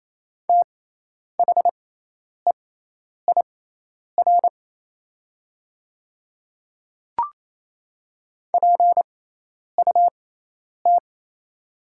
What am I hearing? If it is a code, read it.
Morse code, heard as a keyed tone, about 28 wpm, spaced as for 10 wpm, 690 Hz. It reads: THEIR PUT
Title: 6x Character Spacing - Top 200 Words - 28wpm